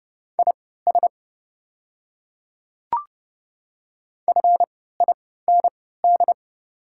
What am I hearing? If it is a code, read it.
Morse code, heard as a keyed tone, about 31 wpm, spaced as for 17 wpm, 700 Hz. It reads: IS FIND
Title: 3x Character Spacing - Top 100 Words - 30wpm